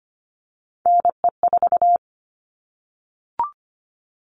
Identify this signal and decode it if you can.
Morse code, heard as a keyed tone, about 25 wpm, 700 Hz. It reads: NE4
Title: All Cut Numbers - 3-Digit 25wpm